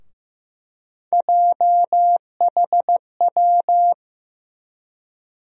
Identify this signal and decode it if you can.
Morse code, heard as a keyed tone, about 15 wpm, 700 Hz. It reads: JHW
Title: USA Call Sign Postfixes - 3 Letters 15wpm